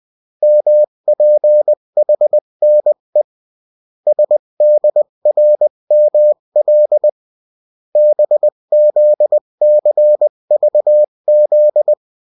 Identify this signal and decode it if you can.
Morse code, heard as a keyed tone, about 20 wpm, 600 Hz. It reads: MPHNE SDRML BZCVZ
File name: morse mp3 / revision9-vitesse 18 mots